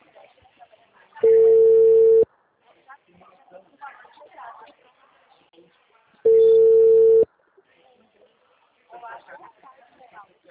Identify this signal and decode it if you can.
Ringback tone (China, 450 Hz), with speech.